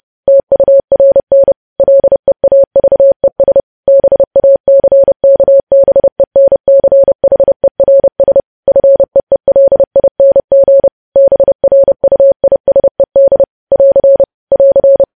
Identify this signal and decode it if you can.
Morse code, heard as a keyed tone, about 30 wpm, 550 Hz. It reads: TURN LEAVES BACKBENCHERS FEELING BRUISED <AR> <AR>
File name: podcast / News headlines-1x-30x30